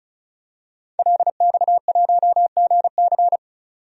Morse code, heard as a keyed tone, about 34 wpm, 700 Hz. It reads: LX1GC
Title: DXCC-100 Call Signs - 5 Characters - 35wpm